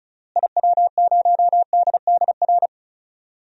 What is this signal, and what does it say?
Morse code, heard as a keyed tone, about 36 wpm, 700 Hz. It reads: IW0DDR
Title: DXCC-100 Call Signs - 6 Characters - Repeated 3x - 35wpm